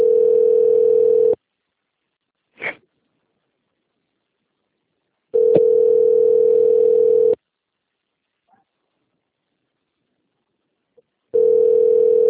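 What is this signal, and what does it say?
Ringback tone (North America), 440 + 480 Hz.
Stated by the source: Call recorded on the Atibaia - SP phone system.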